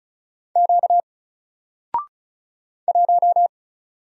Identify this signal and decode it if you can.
Morse code, heard as a keyed tone, about 36 wpm, 700 Hz. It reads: Q 1
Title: Single Letter-Number Alt 35wpm